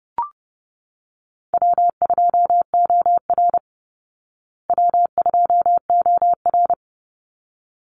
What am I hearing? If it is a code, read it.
Morse code, heard as a keyed tone, about 30 wpm, 700 Hz. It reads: W2OR W2OR
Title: Easy Call Signs Repeated 2x - 30wpm